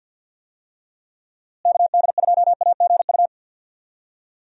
Morse code, heard as a keyed tone, about 47 wpm, 700 Hz. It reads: KD2AGU
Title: USA Call Signs - 6 Characters - Repeated 2x - 50wpm